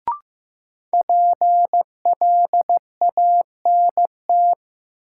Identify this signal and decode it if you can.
Morse code, heard as a keyed tone, about 15 wpm, 700 Hz. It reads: PLANT
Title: Sentences from Top 1000 Words - Review Part 1 - New Word Drill - 15wpm